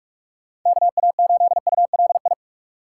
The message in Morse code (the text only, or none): KA8ULI